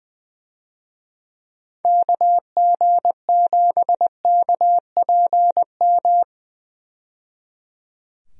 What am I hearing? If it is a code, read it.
Morse code, heard as a keyed tone, about 21 wpm, 700 Hz. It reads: KG7KPM
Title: USA Call Signs - 6 Characters - 20wpm